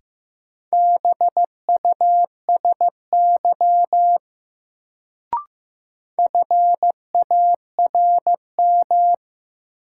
Morse code, heard as a keyed tone, about 15 wpm, 700 Hz. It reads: BUSY FARM
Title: Sentences from Top 1000 Words - Review Part 2 - New Word Drill - 15wpm